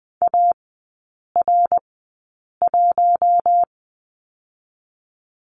Morse code, heard as a keyed tone, about 20 wpm, spaced as for 8 wpm, 700 Hz. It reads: AR1
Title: 053 Lesson 10e - Letter r - Three Characters 20wpm